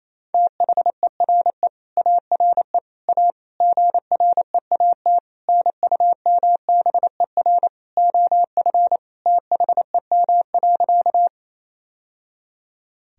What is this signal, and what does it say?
Morse code, heard as a keyed tone, about 28 wpm, 700 Hz. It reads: THERE ARE A GREAT NUMBER OF THEM.
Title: Sentences from Top 1000 Words - Review Part 1 - 28wpm